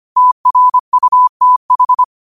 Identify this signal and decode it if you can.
Morse code, heard as a keyed tone, about 25 wpm, 1000 Hz. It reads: TRUTH